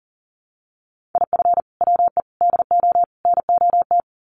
Morse code, heard as a keyed tone, about 40 wpm, 700 Hz. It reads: IF WE DO NOT
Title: Sets of 4 Words 40wpm